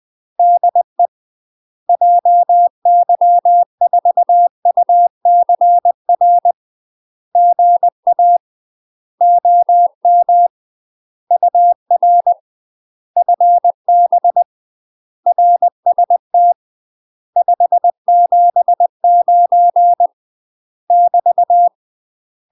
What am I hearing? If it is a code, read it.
Morse code, heard as a keyed tone, about 20 wpm, 700 Hz. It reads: DE JY4UCR GA OM UR FB RST 579 <BT>
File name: UK MASTER QSOs TRAINING / 20WPM / UK MASTER QSOs 0037